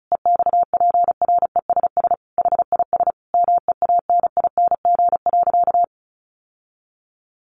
Morse code, heard as a keyed tone, about 34 wpm, 700 Hz. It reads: EXPRESS HIS MEANING.
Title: Sentences from Top 1100 Words 35wpm